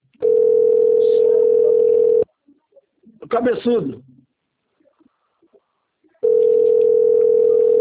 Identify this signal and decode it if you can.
Ringback tone (North America), 440 + 480 Hz, with speech.
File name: rg-1600-unknown- - - .29160